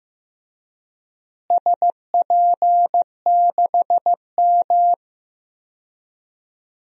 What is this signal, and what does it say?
Morse code, heard as a keyed tone, about 15 wpm, 700 Hz. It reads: SP6M
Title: DXCC-100 Call Signs - 4 Characters - Repeated 2x - 15wpm